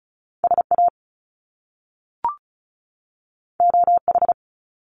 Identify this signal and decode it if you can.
Morse code, heard as a keyed tone, about 36 wpm, 700 Hz. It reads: SA OH
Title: Top Two Letter Combinations in Words 35wpm